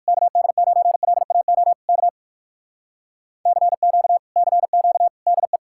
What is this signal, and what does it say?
Morse code, heard as a keyed tone, about 57 wpm, 700 Hz. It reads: KD9LAO K CQ CQ DE